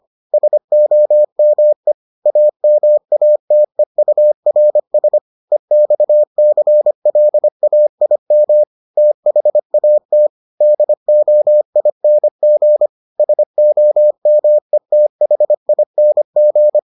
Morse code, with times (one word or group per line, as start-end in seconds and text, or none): 0.3-1.9 SOME
2.3-5.2 AMATEURS
5.5-8.6 EXCLAIM
9.0-10.3 THAT
10.6-12.9 DOING
13.2-16.8 SOMETHING